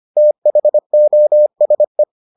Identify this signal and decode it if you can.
Morse code, heard as a keyed tone, about 25 wpm, 600 Hz. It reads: THOSE